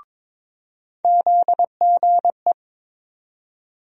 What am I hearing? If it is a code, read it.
Morse code, heard as a keyed tone, about 22 wpm, 700 Hz. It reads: ZGE
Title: USA Call Sign Postfixes - 3 Letters 22wpm